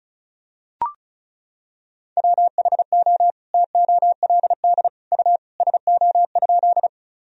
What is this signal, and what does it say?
Morse code, heard as a keyed tone, about 34 wpm, 700 Hz. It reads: WHO TOLD U SO?
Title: Sentences from Top 400 Words and Abbreviations 35wpm